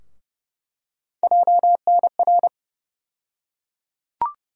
Morse code, heard as a keyed tone, about 30 wpm, 700 Hz. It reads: JNR
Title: USA Call Sign Postfixes - 3 Letters 30wpm